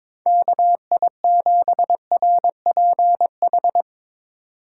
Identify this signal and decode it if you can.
Morse code, heard as a keyed tone, about 22 wpm, 700 Hz. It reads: KI7RPH
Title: USA Call Signs - 6 Characters - 3x Character Spacing - 22wpm